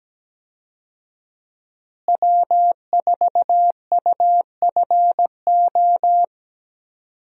Morse code, heard as a keyed tone, about 17 wpm, 700 Hz. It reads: W4UFO
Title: USA Call Signs - 5 Characters - Repeated 2x - 17wpm